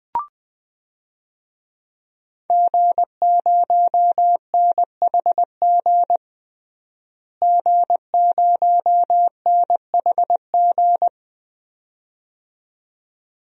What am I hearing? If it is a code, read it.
Morse code, heard as a keyed tone, about 20 wpm, 700 Hz. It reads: G0NHG G0NHG
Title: DXCC-100 Call Signs - 5 Characters - Repeated 2x - 20wpm